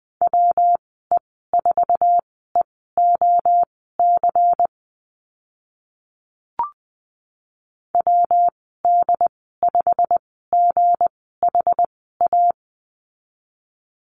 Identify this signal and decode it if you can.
Morse code, heard as a keyed tone, about 20 wpm, spaced as for 15 wpm, 700 Hz. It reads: WE4EOC WD5GHA
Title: 186 Lesson 30g - Letter b - Call Signs 20wpm